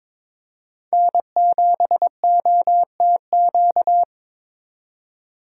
Morse code, heard as a keyed tone, about 27 wpm, 700 Hz. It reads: N7OTQ
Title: Speed Racing - USA Call Signs 22wpm